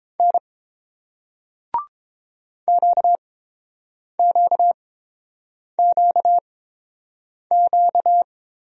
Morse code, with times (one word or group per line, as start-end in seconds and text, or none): 0.2-0.4 N
2.7-8.2 QQQQ